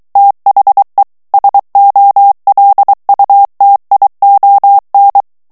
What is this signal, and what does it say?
Morse code, heard as a keyed tone, about 23 wpm, 800 Hz. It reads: THE SOLUTION
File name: morse mp3 / ANGLAIS texte clair 1 vitesse 22 mots minute